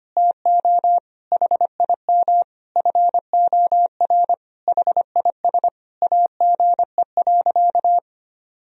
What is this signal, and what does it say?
Morse code, heard as a keyed tone, about 25 wpm, 700 Hz. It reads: TO HIM FOR HIS AGE.
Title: Sentences from Top 1000 Words - Review Part 3 - 25wpm